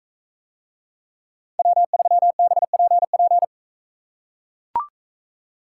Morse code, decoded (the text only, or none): W3BPP